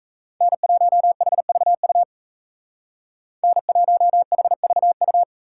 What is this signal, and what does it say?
Morse code, heard as a keyed tone, about 34 wpm, 700 Hz. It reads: N1HVU N1HVU
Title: Speed Racing - USA 1x3 Call Signs 28wpm